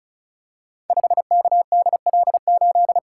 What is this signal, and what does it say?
Morse code, heard as a keyed tone, about 34 wpm, 700 Hz. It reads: 5KDL8
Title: Code Groups - 5 Letter:Numbers - 35wpm